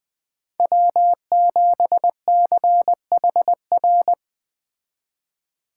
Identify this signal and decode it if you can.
Morse code, heard as a keyed tone, about 20 wpm, 700 Hz. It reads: W7CHR
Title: Speed Racing - USA 1x3 Call Signs 17wpm